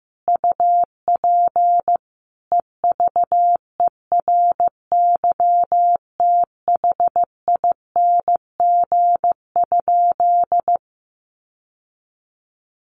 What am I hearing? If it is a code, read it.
Morse code, heard as a keyed tone, about 15 wpm, 700 Hz. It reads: UP EVERYTHING?
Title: Sentences from Top 700 Words 15wpm